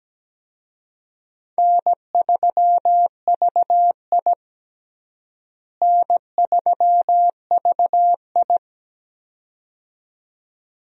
Morse code, heard as a keyed tone, about 17 wpm, 700 Hz. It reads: N3VI N3VI
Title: USA Call Signs - 4 Characters - Repeated 2x - 17wpm